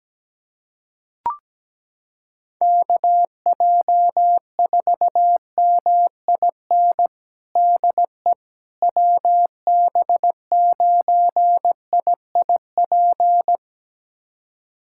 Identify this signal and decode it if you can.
Morse code, heard as a keyed tone, about 17 wpm, 700 Hz. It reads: KJ4MIN DE WB9IIP
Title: USA Callsign DE Callsign - 6 Characters - 17wpm